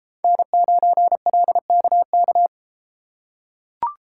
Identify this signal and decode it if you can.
Morse code, heard as a keyed tone, about 33 wpm, 700 Hz. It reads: N9LKK